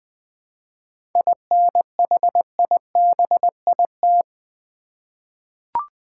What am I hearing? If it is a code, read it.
Morse code, heard as a keyed tone, about 20 wpm, 690 Hz. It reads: INHIBIT
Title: 7 Letter Words - First 3 Letters Given - 20wpm